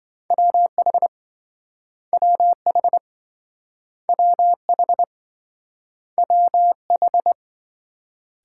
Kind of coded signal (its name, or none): Morse code